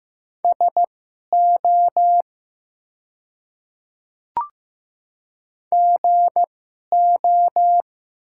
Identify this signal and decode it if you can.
Morse code, heard as a keyed tone, about 15 wpm, spaced as for 11 wpm, 690 Hz. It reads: SO GO
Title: 2x Character Spacing - Top 100 Words - 15wpm